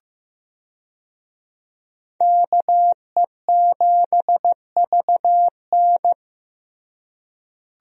Morse code, heard as a keyed tone, about 15 wpm, 700 Hz. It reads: KE7VN
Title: USA Call Signs - 5 Characters - Repeated 2x - 15wpm